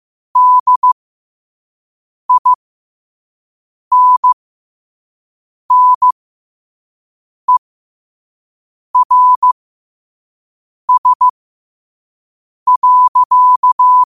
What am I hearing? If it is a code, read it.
Morse code, heard as a keyed tone, about 15 wpm, spaced as for 5 wpm, 1000 Hz. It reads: DINNERS.
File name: 5wpm